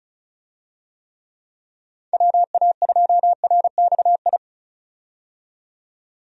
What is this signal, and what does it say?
Morse code, heard as a keyed tone, about 34 wpm, 700 Hz. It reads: WA2RXI
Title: USA Call Signs - 6 Characters - 35wpm